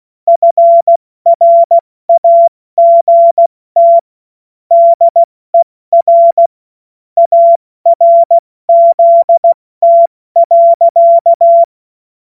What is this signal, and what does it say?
Morse code, heard as a keyed tone, about 16 wpm, 680 Hz. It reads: FRAGT DER ARZT.
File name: Morsekurs / Witze / Witz 13. Juni 2023 70 F80